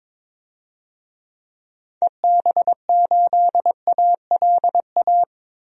Morse code, heard as a keyed tone, about 22 wpm, 700 Hz. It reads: EB8ALA
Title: DXCC-100 Call Signs - 6 Characters - 22wpm